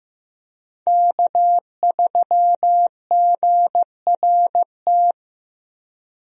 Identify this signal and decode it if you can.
Morse code, heard as a keyed tone, about 16 wpm, 700 Hz. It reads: K3GRT
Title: USA Call Signs - 5 Characters - Repeated 3x - 15wpm